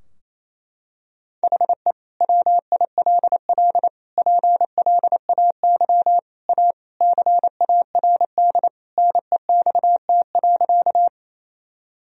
Morse code, heard as a keyed tone, about 27 wpm, 700 Hz. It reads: HE WILL PLAY A CARD NEXT.
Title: Sentences from Top 1000 Words - Review Part 4 - 28wpm